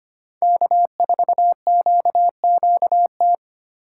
Morse code, heard as a keyed tone, about 25 wpm, 700 Hz. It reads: K4QQT
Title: USA Call Signs - 5 Characters - 2x Character Spacing - 25wpm